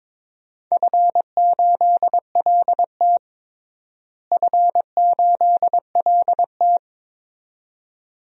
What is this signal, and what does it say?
Morse code, heard as a keyed tone, about 21 wpm, 700 Hz. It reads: F8LT F8LT
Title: DXCC-100 Call Signs - 4 Characters - Repeated 2x - 22wpm